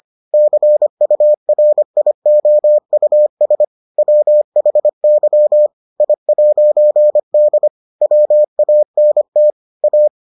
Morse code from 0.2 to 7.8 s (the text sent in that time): CURIOUS WHY I'D